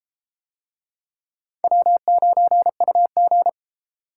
Morse code, heard as a keyed tone, about 30 wpm, 700 Hz. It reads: W9UG